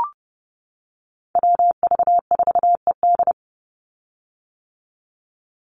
Morse code, heard as a keyed tone, about 30 wpm, 700 Hz. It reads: WV4ED
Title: USA Call Signs 30wpm